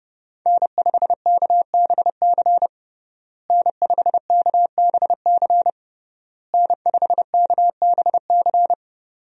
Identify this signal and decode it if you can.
Morse code, heard as a keyed tone, about 30 wpm, 700 Hz. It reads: N5KBC N5KBC N5KBC
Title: USA Call Signs - 5 Characters - Repeated 3x - 30wpm